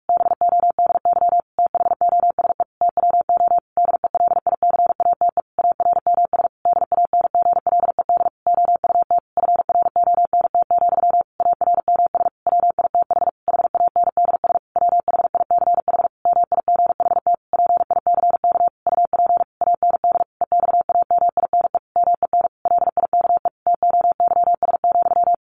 Morse code, heard as a keyed tone, about 44 wpm, 700 Hz. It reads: BODY THOSE TWO DELICATE ARMS DANGLED OUT FRONT, ARMS WITH HANDS WHICH MIGHT PICK UP AND EXAMINE MEN LIKE TOYS,
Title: A Sound Of Thunder 45wpm